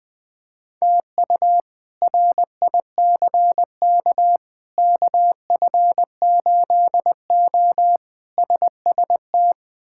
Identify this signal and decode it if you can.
Morse code, heard as a keyed tone, about 20 wpm, 700 Hz. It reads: TU RICK KF8O SST
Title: K1USN SST QSO Practice - 20wpm